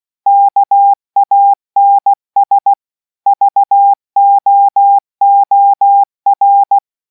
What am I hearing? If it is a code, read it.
Morse code, heard as a keyed tone, about 16 wpm, 800 Hz. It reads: KANS VOOR